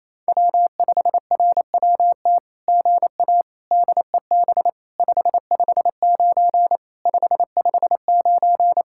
Morse code, heard as a keyed tone, about 28 wpm, 700 Hz. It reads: W5RWT GA DEB 559 559
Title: SOTA QSO Practice 28wpm